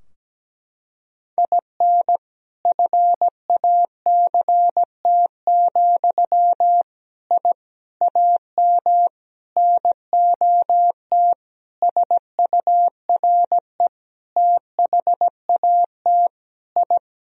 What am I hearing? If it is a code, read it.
Morse code, heard as a keyed tone, about 17 wpm, 700 Hz. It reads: IN FACT, I AM NOT SURE THAT I
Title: Sentences from Top 1000 Words - Review Part 3 - 17wpm